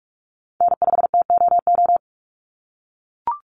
Morse code, heard as a keyed tone, about 44 wpm, 700 Hz. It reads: N5TOO
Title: Call Signs - 45wpm